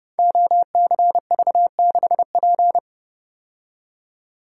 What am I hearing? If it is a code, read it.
Morse code, heard as a keyed tone, about 30 wpm, 700 Hz. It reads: OCV6P
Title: Code Groups - 5 Letter:Numbers - 30wpm